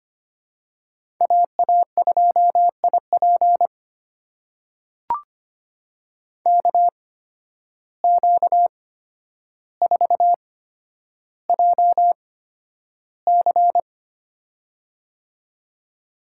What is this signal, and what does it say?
Morse code, heard as a keyed tone, about 25 wpm, spaced as for 11 wpm, 700 Hz. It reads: AA2IP KQ4JC
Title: USA Call Signs - 5 Characters - 8x Character Spacing - 25wpm